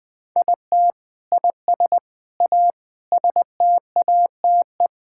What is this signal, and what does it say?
Morse code, heard as a keyed tone, about 20 wpm, 700 Hz. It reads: IT IS A STATE
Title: Sentences from Top 1000 Words - Review Part 5 - 20wpm